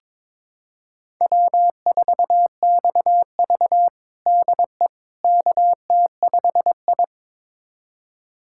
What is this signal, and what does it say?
Morse code, heard as a keyed tone, about 22 wpm, 700 Hz. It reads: W4XV DE KT5I
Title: USA Callsign DE Callsign - 4 Characters - 22wpm